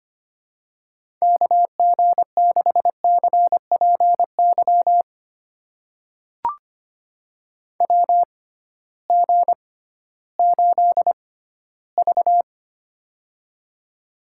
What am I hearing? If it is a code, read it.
Morse code, heard as a keyed tone, about 25 wpm, spaced as for 14 wpm, 700 Hz. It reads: KG6CPY WG8V
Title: USA Call Signs - 6x Character Spacing - 25wpm